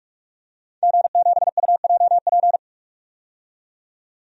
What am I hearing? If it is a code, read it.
Morse code, heard as a keyed tone, about 45 wpm, 700 Hz. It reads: G7UJP
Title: DXCC-100 Call Signs - 5 Characters - 45wpm